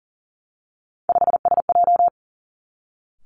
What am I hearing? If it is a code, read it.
Morse code, heard as a keyed tone, about 39 wpm, 700 Hz. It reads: 5SJ
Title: Three Challenging Letter-Number Combinations 40wpm